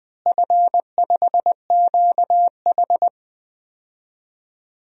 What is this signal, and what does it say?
Morse code, heard as a keyed tone, about 20 wpm, 700 Hz. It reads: F5QH
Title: DXCC-100 Call Signs - 4 Characters - Repeated 3x - 20wpm